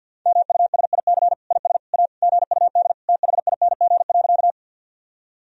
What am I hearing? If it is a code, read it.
Morse code, heard as a keyed tone, about 48 wpm, 700 Hz. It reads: MUSIC IS A GUD THING.
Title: Sentences from Top 300 Words and Abbreviations 50wpm